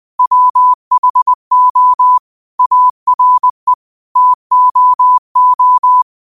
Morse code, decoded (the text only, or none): WHO ARE TOO